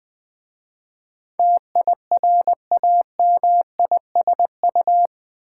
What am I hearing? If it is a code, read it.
Morse code, heard as a keyed tone, about 20 wpm, 690 Hz. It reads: TIRAMISU
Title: Difficult Words with Hints - 20wpm